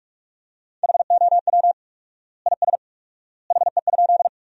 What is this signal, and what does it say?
Morse code, heard as a keyed tone, about 47 wpm, 700 Hz. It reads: HOW IS HE?
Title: 4x Word-Spacing - Sentences from Top 100 Words - 45wpm